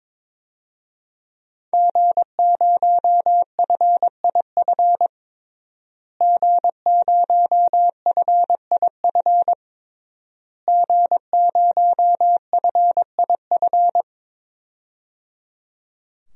Morse code, heard as a keyed tone, about 22 wpm, 700 Hz. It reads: G0FIF G0FIF G0FIF
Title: DXCC-100 Call Signs - 5 Characters - Repeated 3x - 22wpm